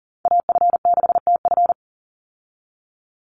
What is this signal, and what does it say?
Morse code, heard as a keyed tone, about 40 wpm, 700 Hz. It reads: AF6TF
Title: USA Call Signs 40wpm